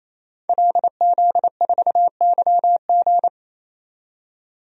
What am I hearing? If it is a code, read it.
Morse code, heard as a keyed tone, about 28 wpm, 700 Hz. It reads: LZ4YG
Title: DXCC-100 Call Signs - 5 Characters - 28wpm